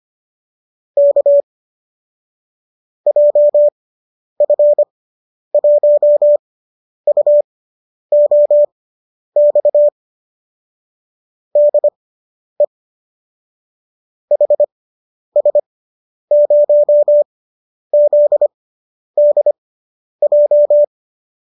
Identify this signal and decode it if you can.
Morse code, heard as a keyed tone, about 25 wpm, spaced as for 10 wpm, 590 Hz. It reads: K JF1UOX DE HS0ZDJ